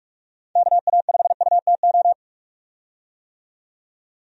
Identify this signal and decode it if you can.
Morse code, heard as a keyed tone, about 42 wpm, 700 Hz. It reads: KA5UTO
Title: USA Call Signs - 6 Characters - 45wpm